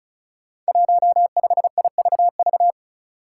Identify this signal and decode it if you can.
Morse code, heard as a keyed tone, about 36 wpm, 700 Hz. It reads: 15IVV
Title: Five Challenging Letter-Number Combinations 35wpm